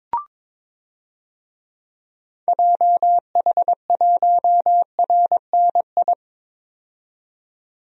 Morse code, heard as a keyed tone, about 21 wpm, 700 Hz. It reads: JH1RNI
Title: DXCC-100 Call Signs - 6 Characters - 22wpm